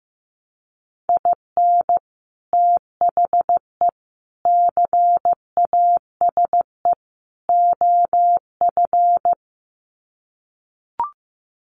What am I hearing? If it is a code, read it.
Morse code, heard as a keyed tone, about 15 wpm, 700 Hz. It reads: IN THE CASE OF
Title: Sets of 4 Words 15wpm